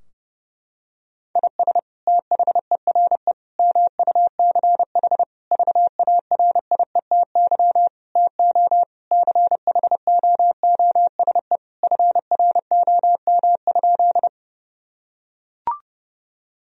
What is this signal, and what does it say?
Morse code, heard as a keyed tone, about 30 wpm, 700 Hz. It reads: IS THERE MUCH VARIETY TO CHOOSE FROM?
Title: Sentences from Top 1000 Words - Review Part 4 - 30wpm